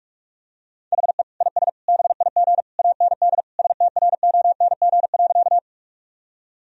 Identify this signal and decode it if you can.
Morse code, heard as a keyed tone, about 41 wpm, 700 Hz. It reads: HE IS BIG AND STRONG.
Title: Sentences from Top 1000 Words - Review Part 3 - 45wpm